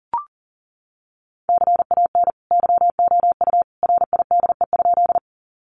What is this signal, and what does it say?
Morse code, heard as a keyed tone, about 40 wpm, 700 Hz. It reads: CAN YOU RIDE?
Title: Sentences from Top 1200 Words 40wpm